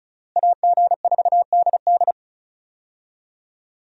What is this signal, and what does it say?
Morse code, heard as a keyed tone, about 36 wpm, 700 Hz. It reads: AG4DD
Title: USA Call Signs - 5 Characters - Repeated 3x - 35wpm